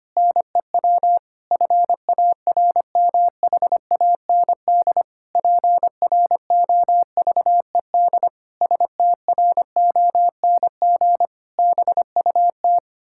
Morse code, heard as a keyed tone, about 25 wpm, 700 Hz. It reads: NEW FARMHAND PROVED STRONG BUT
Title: Pangram Challenge v2 - 25wpm